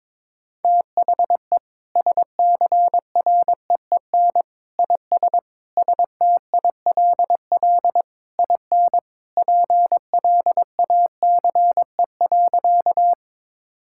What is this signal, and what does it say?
Morse code, heard as a keyed tone, about 22 wpm, 700 Hz. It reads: THE SCREEN IS STILL IN PLACE.
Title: Sentences from Top 1000 Words - Review Part 6 - 22wpm